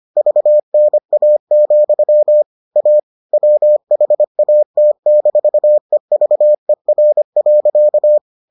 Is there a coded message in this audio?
Yes: Morse code from 0.2 to 8.2 s, heard as a keyed tone, about 25 wpm, 600 Hz. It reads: VNA, A WHAT-EVER.